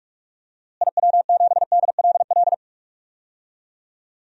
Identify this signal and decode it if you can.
Morse code, heard as a keyed tone, about 46 wpm, 700 Hz. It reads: IW7DLL
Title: DXCC-100 Call Signs - 6 Characters - Repeated 3x - 45wpm